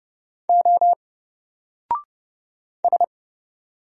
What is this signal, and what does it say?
Morse code, heard as a keyed tone, about 30 wpm, 700 Hz. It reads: O S
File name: Single-Letter-Number-30wpm